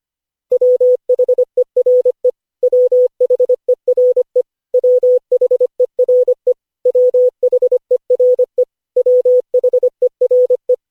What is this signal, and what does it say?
Morse code, heard as a keyed tone, about 25 wpm, 500 Hz. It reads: WHERE WHERE WHERE WHERE WHERE
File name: cw / Session5 cw training